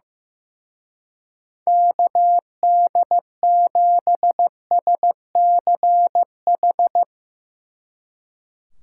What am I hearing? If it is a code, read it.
Morse code, heard as a keyed tone, about 16 wpm, 700 Hz. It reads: KD7SCH
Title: Speed Racing - USA 2x3 Call Signs 15wpm